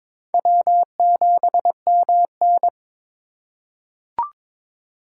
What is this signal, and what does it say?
Morse code, heard as a keyed tone, about 22 wpm, 700 Hz. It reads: W7MN
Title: USA Call Signs - 4x Character Spacing - 22wpm